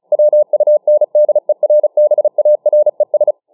Morse code, heard as a keyed tone, about 36 wpm, 600 Hz. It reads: WUNDERBARES